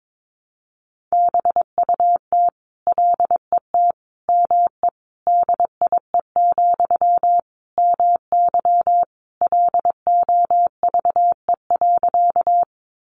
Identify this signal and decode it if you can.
Morse code, heard as a keyed tone, about 22 wpm, 700 Hz. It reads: BUT LET ME DIE, MY LOVE.
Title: Sentences from Top 1300 Words 22wpm